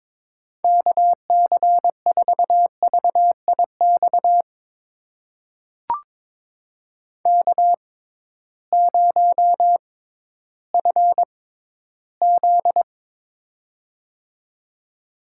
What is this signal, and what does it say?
Morse code, heard as a keyed tone, about 22 wpm, spaced as for 13 wpm, 700 Hz. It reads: KC4VIX K0FZ
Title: USA Call Signs - 6x Character Spacing - 22wpm